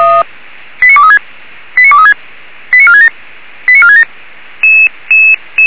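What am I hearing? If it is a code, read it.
Five-tone selective call (ZVEI), heard as quick runs of five tones, tones of about 70 ms. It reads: 89236 89236 89467 89467
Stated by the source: ZVEI five-tone selective call 89383 received by SDR.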